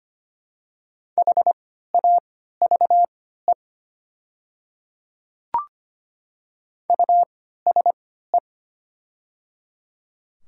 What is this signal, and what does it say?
Morse code, heard as a keyed tone, about 25 wpm, spaced as for 14 wpm, 690 Hz. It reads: HAVE USE
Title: 3x Character Spacing - Top 100 Words - 25wpm